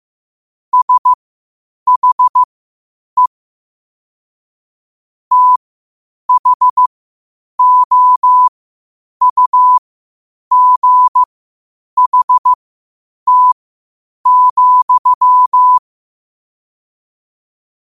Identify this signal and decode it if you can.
Morse code, heard as a keyed tone, about 15 wpm, spaced as for 8 wpm, 1000 Hz. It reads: SHE THOUGHT,
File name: qotdfiles / 8wpm